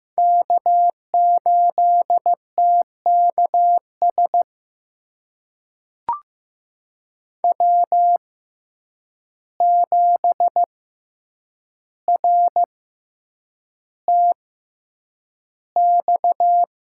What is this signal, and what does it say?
Morse code, heard as a keyed tone, about 15 wpm, spaced as for 8 wpm, 700 Hz. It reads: K8TKS W7RTX
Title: USA Call Signs - 6x Character Spacing - 15wpm